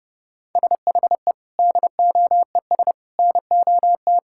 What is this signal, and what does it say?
Morse code, heard as a keyed tone, about 30 wpm, 700 Hz. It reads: SHE DOES NOT
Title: Sentences from Top 1000 Words - Review Part 5 - 30wpm